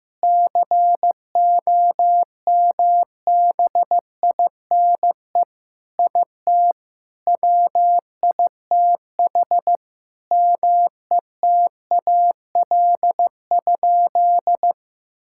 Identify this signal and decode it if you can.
Morse code, heard as a keyed tone, about 15 wpm, 700 Hz. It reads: COMBINE IT WITH METAL?
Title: Sentences from Top 1600 Words 15wpm